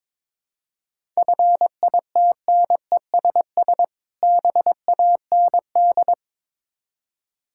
Morse code, heard as a keyed tone, about 23 wpm, 690 Hz. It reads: FITNESS BAND
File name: Difficult Words with Hints - 22wpm